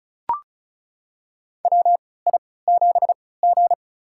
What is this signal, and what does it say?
Morse code, heard as a keyed tone, about 36 wpm, spaced as for 20 wpm, 700 Hz. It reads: WI7G
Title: USA Call Signs - 4 Characters - 3x Character Spacing - 35wpm